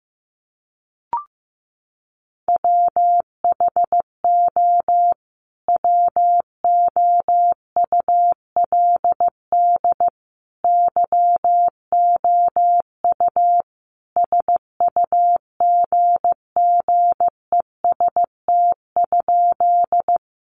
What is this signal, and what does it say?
Morse code, heard as a keyed tone, about 15 wpm, 700 Hz. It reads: WHO WOULD YOU SUGGEST?
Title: Sentences from Top 1300 Words 15wpm